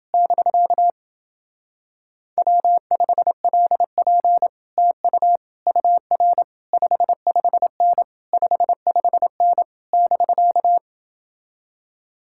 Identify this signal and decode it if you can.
Morse code, heard as a keyed tone, about 30 wpm, 700 Hz. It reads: <BK> W5LP TU UR 55N 55N <BK>